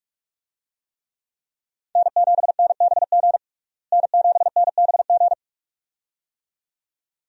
Morse code, heard as a keyed tone, about 46 wpm, 700 Hz. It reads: N7NBG N7NBG
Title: USA Call Signs - 5 Characters - Repeated 2x - 45wpm